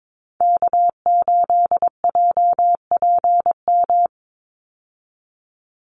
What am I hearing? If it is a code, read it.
Morse code, heard as a keyed tone, about 22 wpm, 700 Hz. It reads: K8JPM
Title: Call Signs - 22wpm